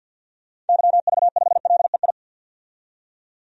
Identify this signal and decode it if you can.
Morse code, heard as a keyed tone, about 47 wpm, 700 Hz. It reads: YV5LEI
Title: DXCC-100 Call Signs - 6 Characters - 50wpm